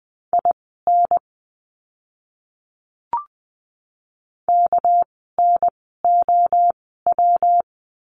Morse code, heard as a keyed tone, about 20 wpm, spaced as for 15 wpm, 700 Hz. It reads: IN KNOW
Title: 237 Lesson 38f - Letter q - Words 20wpm